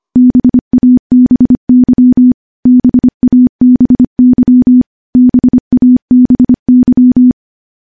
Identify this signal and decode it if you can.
Morse code, heard as a keyed tone, about 25 wpm, 250 Hz. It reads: BABY BABY BABY
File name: morse / morse-training-25WPM